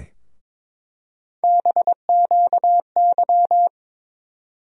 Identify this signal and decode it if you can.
Morse code, heard as a keyed tone, about 22 wpm, 700 Hz. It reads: BQY
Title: USA Call Sign Postfixes - 3 Letters 22wpm